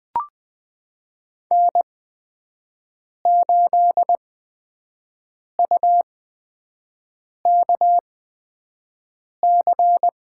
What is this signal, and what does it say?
Morse code, heard as a keyed tone, about 20 wpm, spaced as for 5 wpm, 700 Hz. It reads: N8UKC
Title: USA Call Signs - 8x Character Spacing - 20wpm